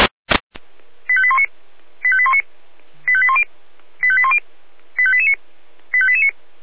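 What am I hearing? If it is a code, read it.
Five-tone selective call (ZVEI), heard as quick runs of five tones, tones of about 70 ms. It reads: 85819 85819 85810 85810 85889 85889
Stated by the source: ZVEI five-tone alert received by SDR.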